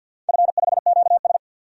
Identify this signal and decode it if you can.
Morse code, heard as a keyed tone, about 48 wpm, 700 Hz. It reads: F5QS
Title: DXCC-100 Call Signs - 4 Characters - 50wpm